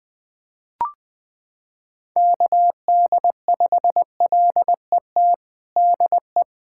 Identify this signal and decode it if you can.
Morse code, heard as a keyed tone, about 20 wpm, 700 Hz. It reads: KD5LET DE
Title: USA Callsign DE Callsign 20wpm